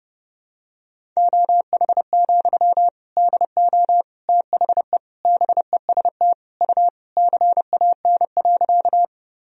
Morse code, heard as a keyed tone, about 30 wpm, 700 Hz. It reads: OH, DO THE BEST U CAN.
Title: Sentences from Top 500 Words and Abbreviations 30wpm